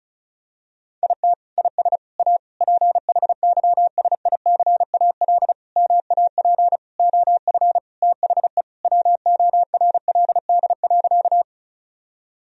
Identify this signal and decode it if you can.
Morse code, heard as a keyed tone, about 34 wpm, 700 Hz. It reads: IT IS A PHYSICAL MAP OF THE WORLD.
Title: Sentences from Top 1000 Words - Review Part 4 - 35wpm